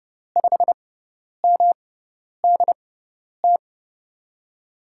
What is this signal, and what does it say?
Morse code, heard as a keyed tone, about 30 wpm, spaced as for 10 wpm, 700 Hz. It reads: 5MDT
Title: USA Call Signs - 6x Character Spacing - 30wpm